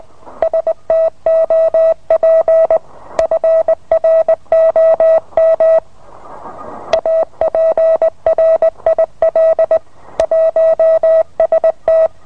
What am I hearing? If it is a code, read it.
Morse code, heard as a keyed tone, about 20 wpm, 650 Hz. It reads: STOP FROM APRIL 1ST